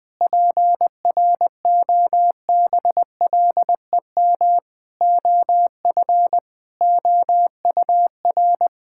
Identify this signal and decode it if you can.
Morse code, heard as a keyed tone, about 20 wpm, 700 Hz. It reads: PROBLEM OF OUR
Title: Sentences from Top 1000 Words - Review Part 5 - 20wpm